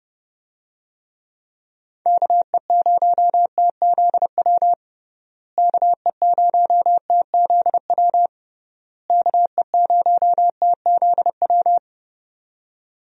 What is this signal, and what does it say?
Morse code, heard as a keyed tone, about 30 wpm, 700 Hz. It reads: KE0TZW KE0TZW KE0TZW
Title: USA Call Signs - 6 Characters - Repeated 3x - 30wpm